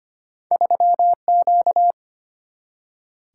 Morse code, heard as a keyed tone, about 34 wpm, spaced as for 30 wpm, 700 Hz. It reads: 3Q
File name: Speed-Racing - Two Letter-Number Combinations - 25wpm